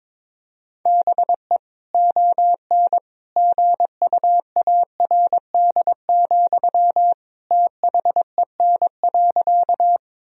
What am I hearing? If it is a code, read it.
Morse code, heard as a keyed tone, about 22 wpm, 700 Hz. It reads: BE ON GUARD, THEN.
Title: Sentences from Top 1000 Words - Review Part 4 - 22wpm